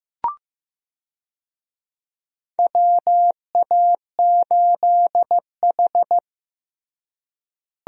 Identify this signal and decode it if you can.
Morse code, heard as a keyed tone, about 15 wpm, 700 Hz. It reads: WA8H
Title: USA Call Signs - 4 Characters - Repeated 3x - 15wpm